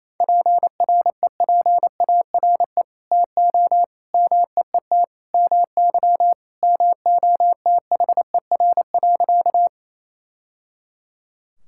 Morse code, heard as a keyed tone, about 28 wpm, 700 Hz. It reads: PREPARE TO MEET MY MOTHER.
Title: Sentences from Top 1000 Words - Review Part 6 - 28wpm